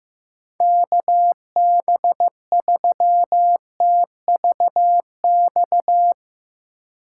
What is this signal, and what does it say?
Morse code, heard as a keyed tone, about 15 wpm, 700 Hz. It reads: KB3TVX
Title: USA Call Signs - 6x Character Spacing - 15wpm